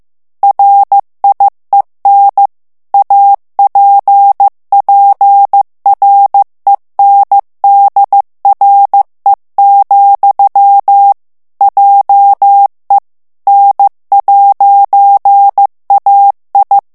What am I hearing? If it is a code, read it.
Morse code, heard as a keyed tone, about 15 wpm, 800 Hz. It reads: RIEN APPRENDRE, JE N'AI
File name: FRANCAIS TEXTE CLAIR 2 VITESSE 14 MOTS MINUTES